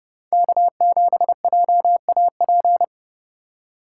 Morse code, heard as a keyed tone, about 29 wpm, 700 Hz. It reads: K7JAP